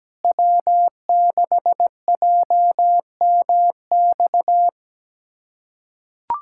Morse code, heard as a keyed tone, about 17 wpm, 700 Hz. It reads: W6JMX
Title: USA Callsign DE Callsign - 5 Characters - 17wpm